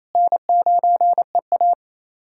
Morse code, heard as a keyed tone, about 28 wpm, 700 Hz. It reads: N9EA
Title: USA Callsign DE Callsign - 4 Characters - 28wpm